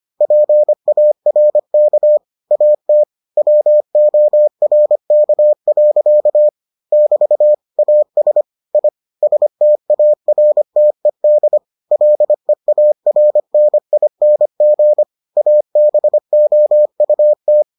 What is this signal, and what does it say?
Morse code, heard as a keyed tone, about 25 wpm, 600 Hz. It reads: PARK AT WORK. <BT> AS I STARTED LEARNING ABOUT